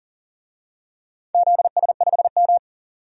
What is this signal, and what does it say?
Morse code, heard as a keyed tone, about 40 wpm, 700 Hz. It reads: ZS5M
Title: DXCC-100 Call Signs - 4 Characters - 40wpm